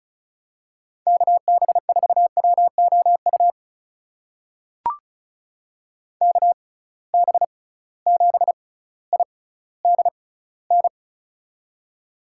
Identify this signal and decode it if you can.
Morse code, heard as a keyed tone, about 36 wpm, spaced as for 18 wpm, 700 Hz. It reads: KB4WOU KB7IDN
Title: USA Call Signs - 6 Characters - 6x Character Spacing - 35wpm